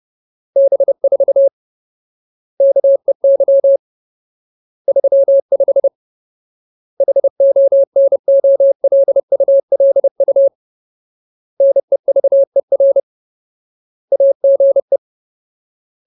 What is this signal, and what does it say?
Morse code, heard as a keyed tone, about 30 wpm, 550 Hz. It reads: B4 KEY 35 HONOLULU NEVER AGE
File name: Ultimate Word List / 30WPM Ultimate